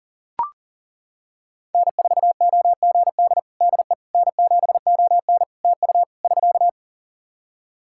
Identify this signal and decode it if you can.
Morse code, heard as a keyed tone, about 40 wpm, 700 Hz. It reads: N4OGD DE N7ON TU <SK>
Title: Practice QSO - Exchange 4 - Ending - 40wpm